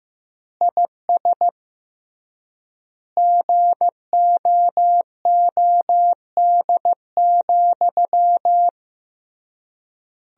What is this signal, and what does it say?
Morse code, heard as a keyed tone, about 15 wpm, 700 Hz. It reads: IS GOOD,
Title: 3x Word-Spacing - Sentences from Top 200 Words - 15wpm